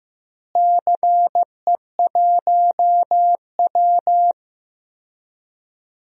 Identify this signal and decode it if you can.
Morse code, heard as a keyed tone, about 15 wpm, 700 Hz. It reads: CE1W
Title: DXCC-100 Call Signs - 4 Characters - Repeated 2x - 15wpm